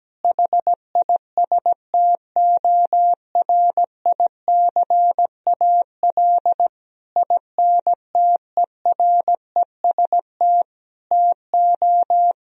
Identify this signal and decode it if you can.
Morse code, heard as a keyed tone, about 17 wpm, 700 Hz. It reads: HISTORICAL INTEREST TO
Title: Sentences from Top 1000 Words - Review Part 4 - 17wpm